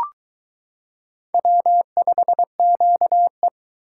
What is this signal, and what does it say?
Morse code, heard as a keyed tone, about 22 wpm, 700 Hz. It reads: W5QE